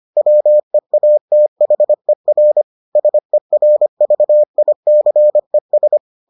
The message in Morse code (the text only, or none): WEATHER SERVICES